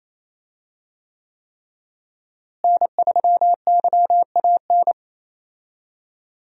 Morse code, heard as a keyed tone, about 28 wpm, 700 Hz. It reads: N3YAN
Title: USA Call Signs - 5 Characters - Repeated 3x - 28wpm